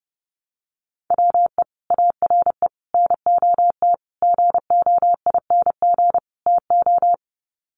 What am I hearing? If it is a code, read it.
Morse code, heard as a keyed tone, about 30 wpm, 700 Hz. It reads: WE ARE NOT GOING TO
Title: Sets of 5 Words 30wpm